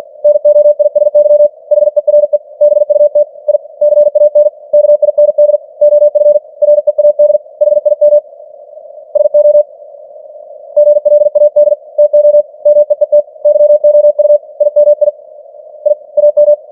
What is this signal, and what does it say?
Morse code, heard as a keyed tone, about 46 wpm, 590 Hz. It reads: NOISY HERE BUT I CAN KIND OF READ HIM SO GLAD TO MEET YOU IMI I AM
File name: archived / CWBros